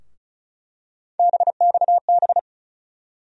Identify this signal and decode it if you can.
Morse code, heard as a keyed tone, about 34 wpm, 700 Hz. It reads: BXB
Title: USA Call Sign Postfixes - 3 Letters 35wpm